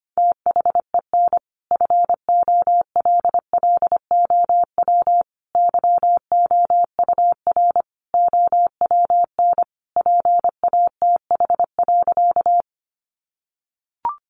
Morse code, heard as a keyed tone, about 25 wpm, 700 Hz. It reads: THEN FOLLOW YOUR OWN PATH.
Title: Sentences from Top 1400 Words 25wpm